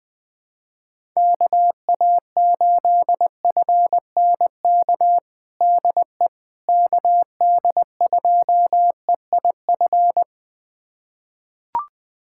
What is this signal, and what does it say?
Morse code, heard as a keyed tone, about 20 wpm, 700 Hz. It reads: KA8FNK DE KD2EIF
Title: USA Callsign DE Callsign - 6 Characters - 20wpm